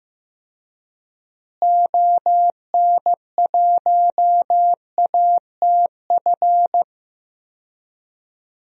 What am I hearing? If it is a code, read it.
Morse code, heard as a keyed tone, about 15 wpm, 700 Hz. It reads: ON1ATF
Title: DXCC-100 Call Signs - 6 Characters - Repeated 3x - 15wpm